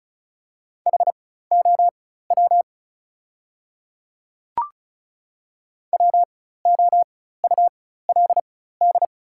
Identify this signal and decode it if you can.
Morse code, heard as a keyed tone, about 36 wpm, spaced as for 16 wpm, 700 Hz. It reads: HOW WOULD
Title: 4x Character Spacing - Top 200 Words - 35wpm